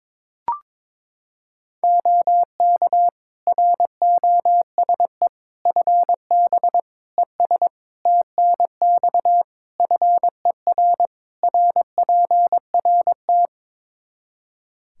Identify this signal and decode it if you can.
Morse code, heard as a keyed tone, about 22 wpm, 700 Hz. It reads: OK ROSE FB ES TNX FER RPRT
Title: Practice QSO - Exchange 2 - Info - 22wpm